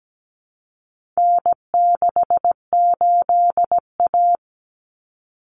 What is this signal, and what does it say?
Morse code, heard as a keyed tone, about 17 wpm, 700 Hz. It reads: N68A
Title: Common Cut Numbers - 4-Digit 17wpm